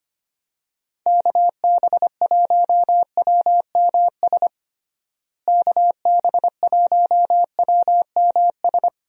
Morse code, heard as a keyed tone, about 25 wpm, 700 Hz. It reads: KB1WMS KB1WMS
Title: USA Call Signs - 6 Characters - Repeated 2x - 25wpm